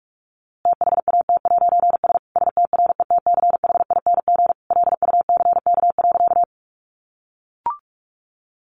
Morse code, heard as a keyed tone, about 44 wpm, 700 Hz. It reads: THAT'S STRETCHING LUCK.
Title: A Sound Of Thunder 45wpm